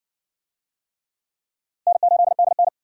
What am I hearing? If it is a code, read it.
Morse code, heard as a keyed tone, about 54 wpm, 700 Hz. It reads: N8DN